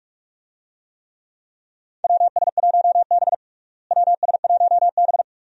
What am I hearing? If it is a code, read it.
Morse code, heard as a keyed tone, about 47 wpm, 700 Hz. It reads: WS1B WS1B
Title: USA Call Signs - 4 Characters - Repeated 2x - 45wpm